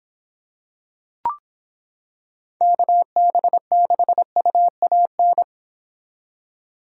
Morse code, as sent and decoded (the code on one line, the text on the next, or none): -.- -... -.... ..- .- -.
KB6UAN